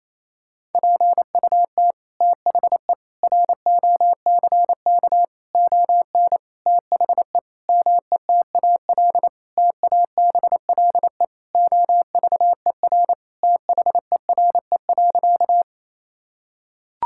Morse code, heard as a keyed tone, about 28 wpm, 700 Hz. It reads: PUT THE ROCK ON THE METAL TABLE OVER THERE.
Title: Sentences from Top 1000 Words - Review Part 6 - 28wpm